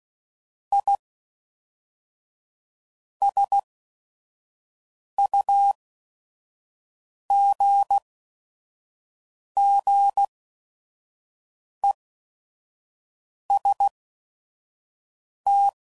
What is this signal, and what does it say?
Morse code, heard as a keyed tone, about 16 wpm, spaced as for 5 wpm, 800 Hz. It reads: I SUGGEST
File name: CW Code Groups @5WPM / Technical Subjects / OR Gate